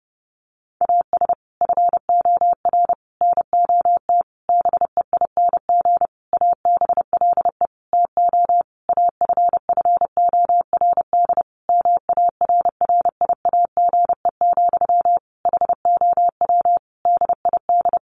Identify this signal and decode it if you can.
Morse code, heard as a keyed tone, about 30 wpm, 700 Hz. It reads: AS FOR NOT BEING ABLE TO AFFORD MARRIAGE, HOW DID
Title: Sentences from Top 1100 Words 30wpm